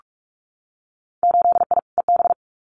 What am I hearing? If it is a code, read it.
Morse code, heard as a keyed tone, about 46 wpm, 700 Hz. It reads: 8I EB
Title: Winter Field Day Exchanges Encore 45wpm